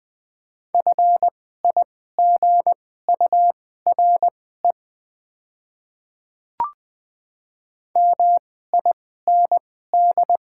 Morse code, heard as a keyed tone, about 20 wpm, spaced as for 14 wpm, 700 Hz. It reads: FIGURE MIND
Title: 2x Character Spacing - Top 500 Words - 20wpm